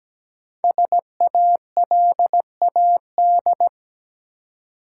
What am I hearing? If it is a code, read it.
Morse code, heard as a keyed tone, about 17 wpm, 700 Hz. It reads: SALAD
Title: Five Letter Words - 17wpm